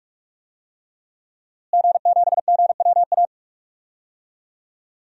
Morse code, heard as a keyed tone, about 46 wpm, 700 Hz. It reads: G7GWA
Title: DXCC-100 Call Signs - 5 Characters - Repeated 3x - 45wpm